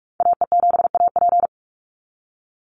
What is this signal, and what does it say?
Morse code, heard as a keyed tone, about 44 wpm, 700 Hz. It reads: AE7AP
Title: NA SOTA Activator Callsigns 45wpm